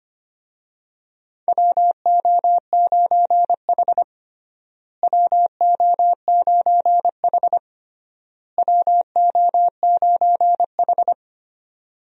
Morse code, heard as a keyed tone, about 25 wpm, 700 Hz. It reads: WO9H WO9H WO9H
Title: USA Call Signs - 4 Characters - Repeated 3x - 25wpm